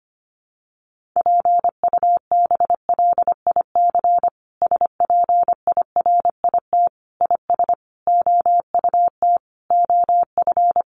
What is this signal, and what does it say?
Morse code, heard as a keyed tone, about 25 wpm, 700 Hz. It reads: PUBLIC SPIRIT IS OUT OF
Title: Sentences from Top 1400 Words 25wpm